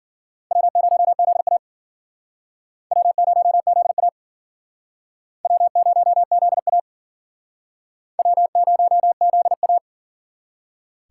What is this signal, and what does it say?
Morse code, heard as a keyed tone, about 54 wpm, 700 Hz. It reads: W0ZA W0ZA W0ZA W0ZA